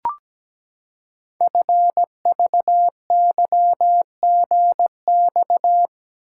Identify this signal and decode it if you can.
Morse code, heard as a keyed tone, about 17 wpm, 700 Hz. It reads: FVYGX
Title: Code Groups - 5 Letters - 17wpm